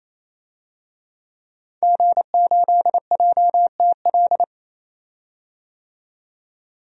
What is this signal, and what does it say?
Morse code, heard as a keyed tone, about 28 wpm, 700 Hz. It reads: G8JTL
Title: DXCC-100 Call Signs - 5 Characters - 28wpm